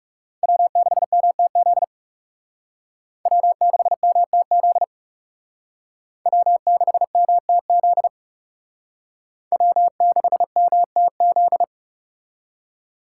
Morse code, heard as a keyed tone, about 37 wpm, 700 Hz. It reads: W6MTZ W6MTZ W6MTZ W6MTZ
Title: Speed Racing - USA 1x3 Call Signs 30wpm